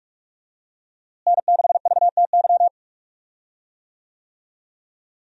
Morse code, heard as a keyed tone, about 42 wpm, 700 Hz. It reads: N6VTY
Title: USA Call Signs - 5 Characters - Repeated 3x - 45wpm